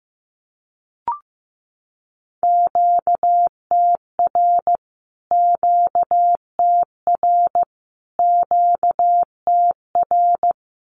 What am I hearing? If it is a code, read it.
Morse code, heard as a keyed tone, about 15 wpm, 700 Hz. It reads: QTR QTR QTR
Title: Q-Codes Repeated 3x 15wpm